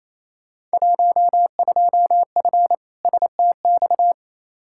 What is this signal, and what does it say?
Morse code, heard as a keyed tone, about 28 wpm, 700 Hz. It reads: 12F STX
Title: Field Day Exchanges Encore - 28wpm